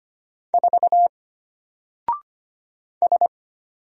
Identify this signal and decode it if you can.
Morse code, heard as a keyed tone, about 25 wpm, 700 Hz. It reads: 4 S
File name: Single-Letter-Number-25wpm